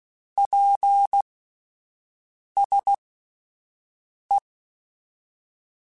Morse code, heard as a keyed tone, about 16 wpm, spaced as for 5 wpm, 800 Hz. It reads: PSE
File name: QSOs Spaced